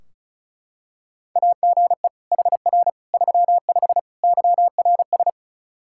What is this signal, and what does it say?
Morse code, heard as a keyed tone, about 36 wpm, 700 Hz. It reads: AGE HR 35 YRS
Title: Practice QSO - Exchange 3 - More Info - 35wpm